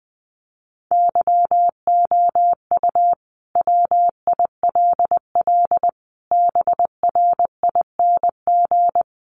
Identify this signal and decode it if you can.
Morse code, heard as a keyed tone, about 20 wpm, 700 Hz. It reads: YOU WILL BRING
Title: Sentences from Top 1300 Words 20wpm